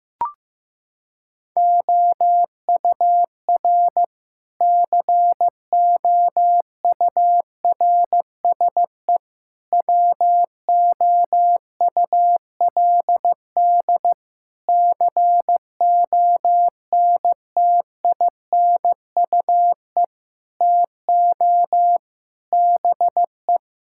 Morse code, heard as a keyed tone, about 16 wpm, 700 Hz. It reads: OUR COURSE WOULD CONTINUE TO BE
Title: Sentences from Top 1000 Words - Review Part 6 - 15wpm